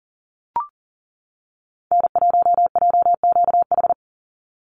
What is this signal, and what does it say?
Morse code, heard as a keyed tone, about 40 wpm, 700 Hz. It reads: N1JQH
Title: USA Call Signs 40wpm